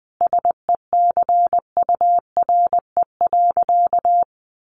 Morse code, heard as a keyed tone, about 20 wpm, 700 Hz. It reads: SECURE.
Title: Sentences from Top 1500 Words 20wpm